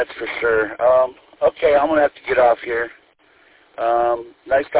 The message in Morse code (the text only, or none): none